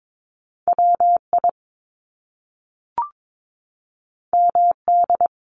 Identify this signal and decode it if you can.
Morse code, heard as a keyed tone, about 22 wpm, 700 Hz. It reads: WI MD
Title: US States 22wpm